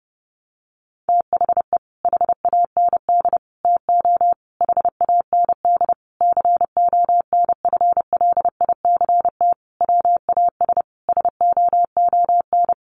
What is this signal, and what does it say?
Morse code, heard as a keyed tone, about 30 wpm, 700 Hz. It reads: THE HAND TO HAND CONFLICT WAS SOON
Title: Sentences from Top 1300 Words 30wpm